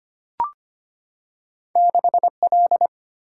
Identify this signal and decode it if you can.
Morse code, heard as a keyed tone, about 25 wpm, 700 Hz. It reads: 6L
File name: Two Letter-Number Combinations - 25wpm